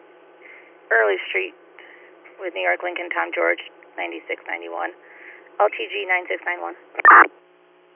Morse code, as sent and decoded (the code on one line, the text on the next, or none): none